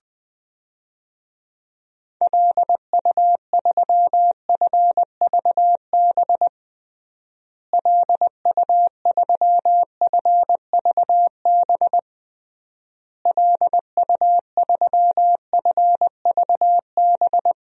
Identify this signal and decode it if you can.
Morse code, heard as a keyed tone, about 20 wpm, 700 Hz. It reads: LU3FVB LU3FVB LU3FVB
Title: DXCC-100 Call Signs - 6 Characters - Repeated 3x - 20wpm